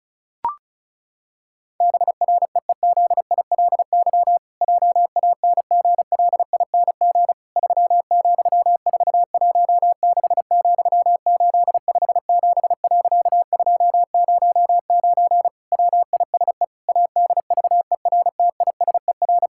Morse code, heard as a keyed tone, about 36 wpm, 700 Hz. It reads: BREEZILY JANGLING 3,416,857.209 WISE ADVERTISER
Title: Pangram Challenge v2 - 35wpm